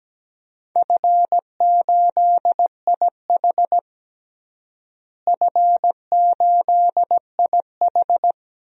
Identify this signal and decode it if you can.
Morse code, heard as a keyed tone, about 17 wpm, 700 Hz. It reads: F8IH F8IH
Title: DXCC-100 Call Signs - 4 Characters - Repeated 3x - 17wpm